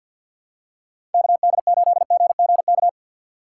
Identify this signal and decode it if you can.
Morse code, heard as a keyed tone, about 52 wpm, 700 Hz. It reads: KD8GGK
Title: USA Call Signs - 6 Characters - 4x Character Spacing - 50wpm